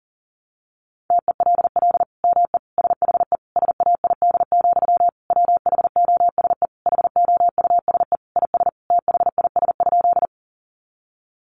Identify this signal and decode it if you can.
Morse code, heard as a keyed tone, about 40 wpm, 700 Hz. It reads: TELL ME SHE SAID, WHOSE HOUSE IS THIS?
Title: Sentences from Top 1200 Words 40wpm